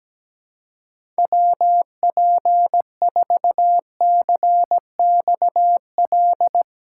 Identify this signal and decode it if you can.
Morse code, heard as a keyed tone, about 17 wpm, 700 Hz. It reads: WP4CXL
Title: USA Call Signs - 6 Characters - 4x Character Spacing - 17wpm